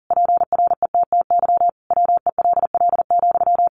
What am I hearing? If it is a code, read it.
Morse code, heard as a keyed tone, about 40 wpm, 700 Hz. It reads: PRETTY WELL,
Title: Sentences from Top 1000 Words 40wpm